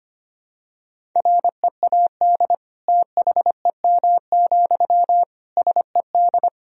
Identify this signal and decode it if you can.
Morse code, heard as a keyed tone, about 25 wpm, 700 Hz. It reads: READ THEM, SED
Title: Sentences from Top 400 Words and Abbreviations 25wpm